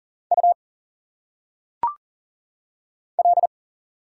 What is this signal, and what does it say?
Morse code, heard as a keyed tone, about 40 wpm, 700 Hz. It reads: U L
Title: Single Letters - Phonetic Pronunciation 40wpm